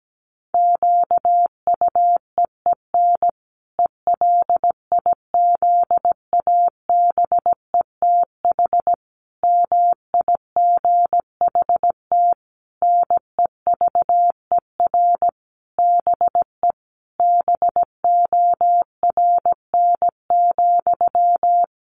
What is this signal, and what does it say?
Morse code, heard as a keyed tone, about 17 wpm, 700 Hz. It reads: QUEEN ELIZABETH MIGHT NEVER BE BORN,
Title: A Sound Of Thunder 17wpm